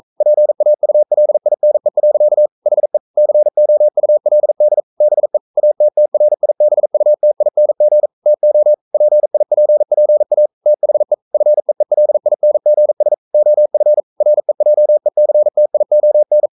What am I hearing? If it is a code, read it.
Morse code, heard as a keyed tone, about 42 wpm, 600 Hz. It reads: PAULINE. HE COULD BE ATTRIBUTING TO PIPPA THE FEELINGS OF REJECTION
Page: Romanen Never af Ken Follet på engelsk.